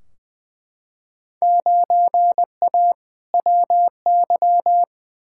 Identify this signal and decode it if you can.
Morse code, heard as a keyed tone, about 20 wpm, 700 Hz. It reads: 9A WY
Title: Field Day Exchanges Encore - 20wpm